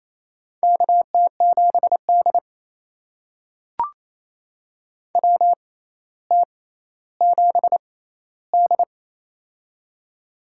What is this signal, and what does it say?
Morse code, heard as a keyed tone, about 28 wpm, spaced as for 14 wpm, 700 Hz. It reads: KT7D WT7D
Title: USA Call Signs - 4 Characters - 6x Character Spacing - 28wpm